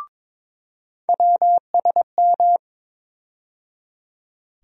Morse code, heard as a keyed tone, about 22 wpm, 700 Hz. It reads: WSM
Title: USA Call Sign Postfixes - 3 Letters 22wpm